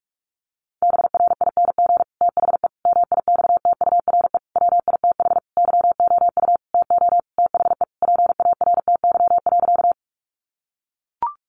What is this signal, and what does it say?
Morse code, heard as a keyed tone, about 44 wpm, 700 Hz. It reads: BRING THE MIXTURE WITH YOU TO THE PARTY.
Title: Sentences from Top 1500 Words 45wpm